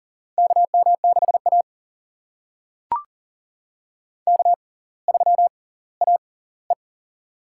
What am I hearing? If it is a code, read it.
Morse code, heard as a keyed tone, about 40 wpm, spaced as for 21 wpm, 700 Hz. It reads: KM6A K3AE
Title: USA Call Signs - 6x Character Spacing - 40wpm